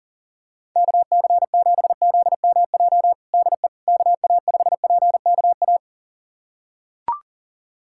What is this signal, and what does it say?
Morse code, heard as a keyed tone, about 37 wpm, 700 Hz. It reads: KC7ZMJ DE KA5PKA
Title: USA Callsign DE Callsign - 6 Characters - 40wpm